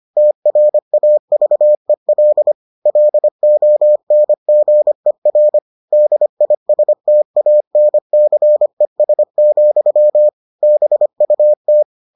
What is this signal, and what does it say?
Morse code, heard as a keyed tone, about 25 wpm, 600 Hz. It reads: TRAVEL LONGER DISTANCES, BUT